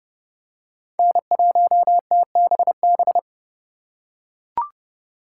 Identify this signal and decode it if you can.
Morse code, heard as a keyed tone, about 30 wpm, 700 Hz. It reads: N1TBB
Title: USA Call Signs - 5 Characters - 3x Character Spacing - 30wpm